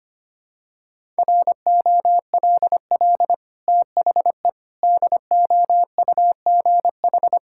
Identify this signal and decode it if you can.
Morse code, heard as a keyed tone, about 25 wpm, 690 Hz. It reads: ROLL THE DOUGH
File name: Easy Sentences with Hints - 25wpm